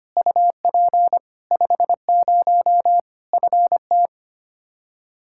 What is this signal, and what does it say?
Morse code, heard as a keyed tone, about 25 wpm, 700 Hz. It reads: UP 50 FT
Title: Practice QSO - Exchange 2 - Info - 25wpm